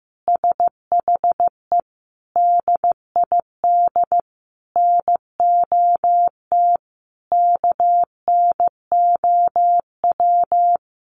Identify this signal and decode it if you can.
Morse code, heard as a keyed tone, about 15 wpm, 700 Hz. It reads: SHE DID NOT KNOW
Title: Sentences from Top 1300 Words 15wpm